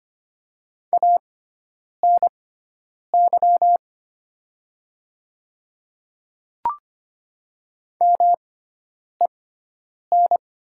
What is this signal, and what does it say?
Morse code, heard as a keyed tone, about 25 wpm, spaced as for 9 wpm, 690 Hz. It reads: ANY MEN
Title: 6x Character Spacing - Top 200 Words - 25wpm